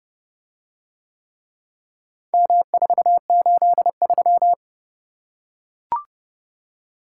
Morse code, heard as a keyed tone, about 29 wpm, 700 Hz. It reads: M483